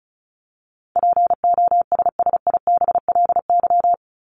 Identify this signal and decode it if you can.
Morse code, heard as a keyed tone, about 34 wpm, 700 Hz. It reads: POSSIBLY
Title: LY Suffix - Top 100 Words with Base Words 35wpm